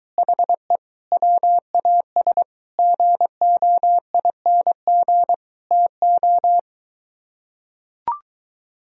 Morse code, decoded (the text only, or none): HE WAS GOING TO